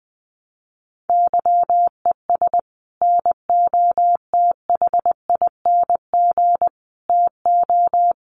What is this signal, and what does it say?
Morse code, heard as a keyed tone, about 20 wpm, 700 Hz. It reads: YES NOTHING TO
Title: Sentences from Top 1300 Words 20wpm